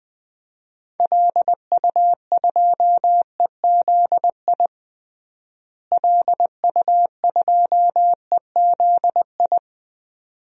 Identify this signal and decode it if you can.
Morse code, heard as a keyed tone, about 21 wpm, 700 Hz. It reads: LU2EZI LU2EZI
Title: DXCC-100 Call Signs - 6 Characters - Repeated 2x - 20wpm